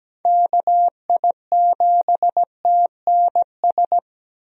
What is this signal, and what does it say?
Morse code, heard as a keyed tone, about 17 wpm, 700 Hz. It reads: KI7TNS
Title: USA Call Signs - 6 Characters - Repeated 3x - 17wpm